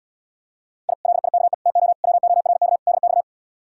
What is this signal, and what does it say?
Morse code, heard as a keyed tone, about 78 wpm, 700 Hz. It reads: I LIKE TO WORK DX